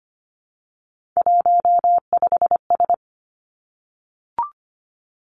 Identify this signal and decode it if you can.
Morse code, heard as a keyed tone, about 25 wpm, 700 Hz. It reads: 15S
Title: Three Challenging Letter-Number Combinations 25wpm